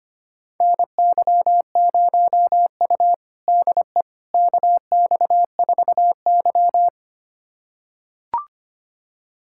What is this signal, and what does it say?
Morse code, heard as a keyed tone, about 25 wpm, 700 Hz. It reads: NY0U DE KX4Y
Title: USA Callsign DE Callsign - 4 Characters - 25wpm